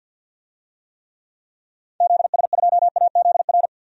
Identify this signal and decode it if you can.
Morse code, heard as a keyed tone, about 47 wpm, 700 Hz. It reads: ZS2AZR
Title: DXCC-100 Call Signs - 6 Characters - 50wpm